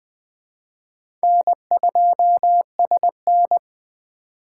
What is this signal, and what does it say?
Morse code, heard as a keyed tone, about 24 wpm, 700 Hz. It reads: N2SN
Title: Speed Racing - USA 1x2 Call Signs 20wpm